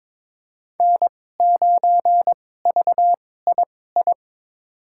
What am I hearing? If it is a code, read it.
Morse code, heard as a keyed tone, about 22 wpm, spaced as for 16 wpm, 700 Hz. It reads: N9VII
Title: USA Call Signs - 2x Character Spacing - 22wpm